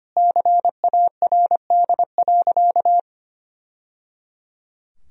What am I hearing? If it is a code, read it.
Morse code, heard as a keyed tone, about 25 wpm, 700 Hz. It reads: CARD.
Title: Sentences from Top 1000 Words - Review Part 4 - 25wpm